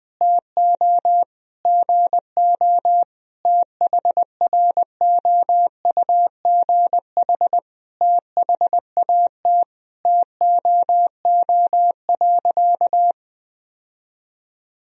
Morse code, heard as a keyed tone, about 20 wpm, 700 Hz. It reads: TO GO THROUGH THAT TOO.
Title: Sentences from Top 1000 Words - Review Part 1 - 20wpm